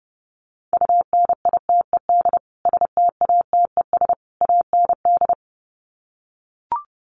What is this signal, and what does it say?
Morse code, heard as a keyed tone, about 28 wpm, 700 Hz. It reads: UNITED STATES AND
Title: Sets of 3 Words 30wpm